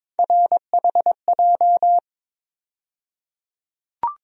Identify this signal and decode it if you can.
Morse code, heard as a keyed tone, about 22 wpm, 700 Hz. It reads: RHJ
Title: USA Call Sign Postfixes - 3 Letters 22wpm